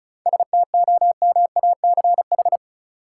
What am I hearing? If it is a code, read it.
Morse code, heard as a keyed tone, about 34 wpm, 700 Hz. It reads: STOMACH
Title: Seven Letter Words - 35wpm